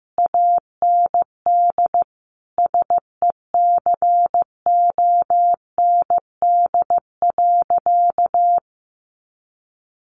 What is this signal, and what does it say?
Morse code, heard as a keyed tone, about 15 wpm, 700 Hz. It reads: AND SECOND.
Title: A Sound Of Thunder 15wpm